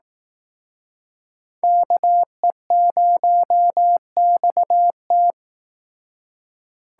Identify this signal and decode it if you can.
Morse code, heard as a keyed tone, about 18 wpm, 700 Hz. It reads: KE0XT
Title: Speed Racing - USA 2x2 Call Signs 15wpm